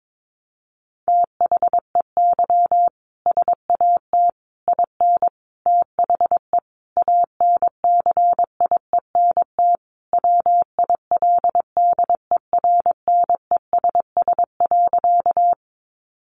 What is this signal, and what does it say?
Morse code, heard as a keyed tone, about 22 wpm, 700 Hz. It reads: THEY SAT IN THE ANCIENT WILDERNESS.
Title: A Sound Of Thunder 22wpm